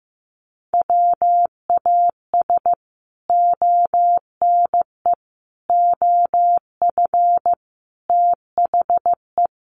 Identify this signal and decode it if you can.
Morse code, heard as a keyed tone, about 15 wpm, 700 Hz. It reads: WAS ONE OF THE
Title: Sets of 5 Words 15wpm